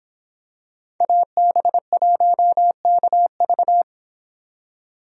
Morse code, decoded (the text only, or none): AB1KV